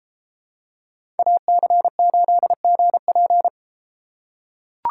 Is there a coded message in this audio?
Yes: Morse code from 1.2 to 3.5 s, heard as a keyed tone, about 30 wpm, 700 Hz. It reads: AC8GP